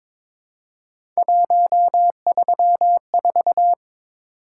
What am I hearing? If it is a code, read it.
Morse code, heard as a keyed tone, about 23 wpm, 700 Hz. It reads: 134
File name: practice / 22wpm / Three Letter-Number Combinations - 22wpm